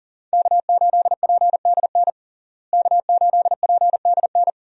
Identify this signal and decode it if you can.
Morse code, heard as a keyed tone, about 40 wpm, 700 Hz. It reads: K8PDN K8PDN
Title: USA Call Signs - 5 Characters - Repeated 2x - 40wpm